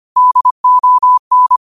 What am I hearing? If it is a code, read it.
Morse code, heard as a keyed tone, about 25 wpm, 1000 Hz. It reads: DON